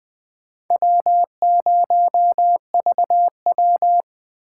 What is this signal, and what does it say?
Morse code, heard as a keyed tone, about 20 wpm, 700 Hz. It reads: W0VW
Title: USA Call Signs - 4 Characters - 3x Character Spacing - 20wpm